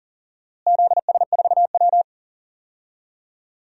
Morse code, heard as a keyed tone, about 40 wpm, 700 Hz. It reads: ZS4W
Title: DXCC-100 Call Signs - 4 Characters - Repeated 2x - 40wpm